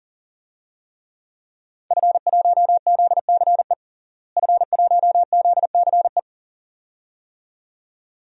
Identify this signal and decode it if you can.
Morse code, heard as a keyed tone, about 40 wpm, 700 Hz. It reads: F1ZCE F1ZCE
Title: DXCC-100 Call Signs - 5 Characters - Repeated 2x - 40wpm